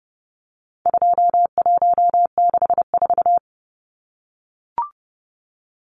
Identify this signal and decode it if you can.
Morse code, heard as a keyed tone, about 30 wpm, 700 Hz. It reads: 2164
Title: Numbers - 4-Digit 30wpm